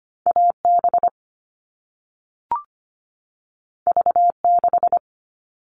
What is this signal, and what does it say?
Morse code, heard as a keyed tone, about 25 wpm, 700 Hz. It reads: AB V6
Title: All Cut Numbers - 2-Digit 25wpm